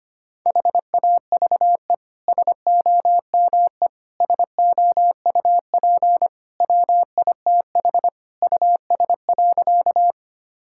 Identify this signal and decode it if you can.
Morse code, heard as a keyed tone, about 25 wpm, 700 Hz. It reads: HAVE SOME SOUP WITH US.
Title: Sentences from Top 1000 Words - Review Part 6 - 25wpm